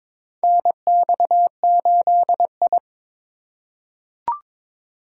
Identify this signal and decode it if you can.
Morse code, heard as a keyed tone, about 23 wpm, 700 Hz. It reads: NX8I
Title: USA Call Signs - 4 Characters - 2x Character Spacing - 22wpm